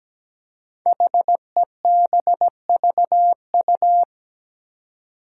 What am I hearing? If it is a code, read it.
Morse code, heard as a keyed tone, about 17 wpm, 700 Hz. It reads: HEBVU
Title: Five Challenging Letter-Number Combinations 17wpm